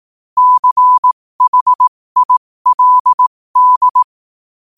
Morse code, heard as a keyed tone, about 18 wpm, spaced as for 15 wpm, 1000 Hz. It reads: CHILD
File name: qotdfiles / 15wpm